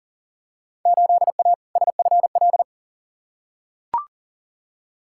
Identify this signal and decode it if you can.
Morse code, heard as a keyed tone, about 40 wpm, 700 Hz. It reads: 8A SFL
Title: Field Day Exchanges Encore - 40wpm